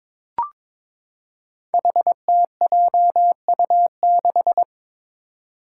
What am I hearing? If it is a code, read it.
Morse code, heard as a keyed tone, about 22 wpm, 700 Hz. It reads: HTJU6
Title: Five Challenging Letter-Number Combinations 22wpm